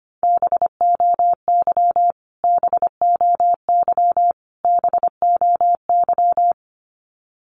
Morse code, heard as a keyed tone, about 25 wpm, 700 Hz. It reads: BOY BOY BOY
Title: Top 500 Words Repeated 25wpm